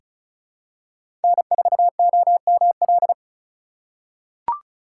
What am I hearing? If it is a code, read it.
Morse code, heard as a keyed tone, about 34 wpm, 700 Hz. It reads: N4OML
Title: USA Call Signs - 5 Characters - 4x Character Spacing - 35wpm